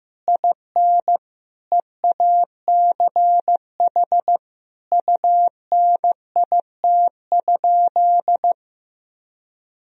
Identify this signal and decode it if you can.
Morse code, heard as a keyed tone, about 14 wpm, 700 Hz. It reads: IN EACH UNIT?
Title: Sentences from Top 1000 Words - Review Part 2 - 15wpm